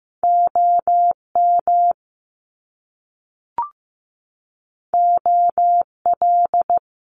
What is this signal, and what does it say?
Morse code, heard as a keyed tone, about 15 wpm, 700 Hz. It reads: OM OL
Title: Top Two Letter Combinations in Words 15wpm